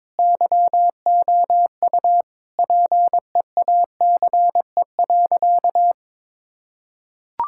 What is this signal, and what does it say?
Morse code, heard as a keyed tone, about 22 wpm, 700 Hz. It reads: YOU PEACE.
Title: Sentences from Top 1600 Words 22wpm